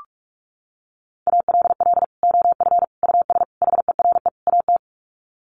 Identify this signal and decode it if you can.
Morse code, heard as a keyed tone, about 44 wpm, 700 Hz. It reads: ALL OF US HERE AT
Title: Sets of 5 Words 45wpm